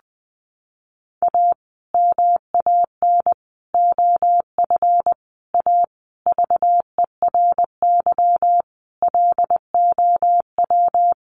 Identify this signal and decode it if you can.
Morse code, heard as a keyed tone, about 20 wpm, 700 Hz. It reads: A MAN OF A VERY LOW
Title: Sentences from Top 1200 Words 20wpm